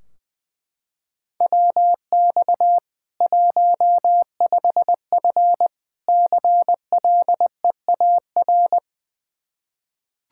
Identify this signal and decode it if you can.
Morse code, heard as a keyed tone, about 20 wpm, 700 Hz. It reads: WX 15F CLEAR
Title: Practice QSO - Exchange 2 - Info - 20wpm